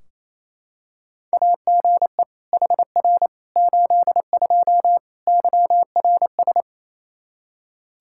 Morse code, heard as a keyed tone, about 28 wpm, 700 Hz. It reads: AGE HR 82 YRS
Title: K1USN SST QSO Practice 28wpm